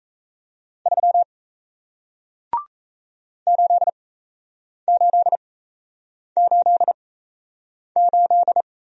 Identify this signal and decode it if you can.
Morse code, heard as a keyed tone, about 37 wpm, spaced as for 8 wpm, 700 Hz. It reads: 3 8888